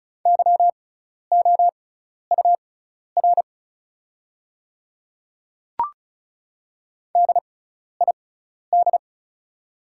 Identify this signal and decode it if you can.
Morse code, heard as a keyed tone, about 36 wpm, spaced as for 12 wpm, 700 Hz. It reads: YOUR DID
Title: 6x Character Spacing - Top 100 Words - 35wpm